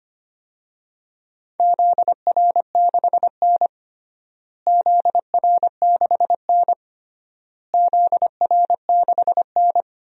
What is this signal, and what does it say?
Morse code, heard as a keyed tone, about 25 wpm, 700 Hz. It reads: ZR6N ZR6N ZR6N
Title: DXCC-100 Call Signs - 4 Characters - Repeated 3x - 25wpm